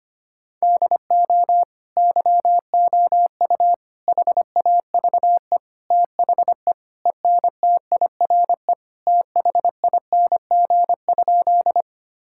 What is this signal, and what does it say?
Morse code, heard as a keyed tone, about 25 wpm, 700 Hz. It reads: DO YOU HAVE THE ENTIRE THING?
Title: Sentences from Top 1000 Words - Review Part 5 - 25wpm